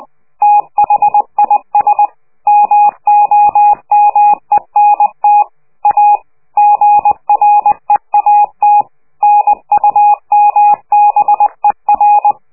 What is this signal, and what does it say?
Morse code, heard as a keyed tone, about 20 wpm, 830 Hz. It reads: THIS MOMENT A GREAT NUMBER
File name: worldcoastal / scheveningenradio / morsepch